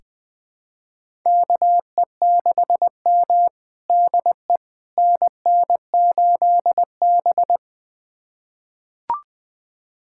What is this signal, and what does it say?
Morse code, heard as a keyed tone, about 20 wpm, 700 Hz. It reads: KE6M DE NN8B
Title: USA Callsign DE Callsign - 4 Characters - 20wpm